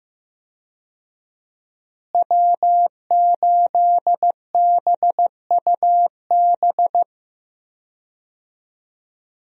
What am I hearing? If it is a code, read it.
Morse code, heard as a keyed tone, about 15 wpm, 700 Hz. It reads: W8BUB
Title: USA Call Signs - 5 Characters - Repeated 2x - 15wpm